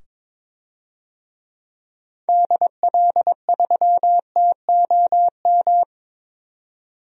Morse code, heard as a keyed tone, about 22 wpm, 700 Hz. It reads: DL3TOM
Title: DXCC-100 Call Signs - 6 Characters - 22wpm